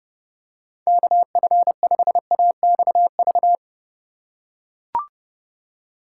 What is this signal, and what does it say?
Morse code, heard as a keyed tone, about 32 wpm, 700 Hz. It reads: KF5AXV
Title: USA Call Signs - 2x Character Spacing - 30wpm